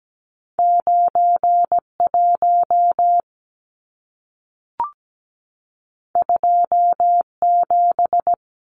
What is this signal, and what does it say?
Morse code, heard as a keyed tone, about 17 wpm, 700 Hz. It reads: 91 27
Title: Numbers - 2-Digit 17wpm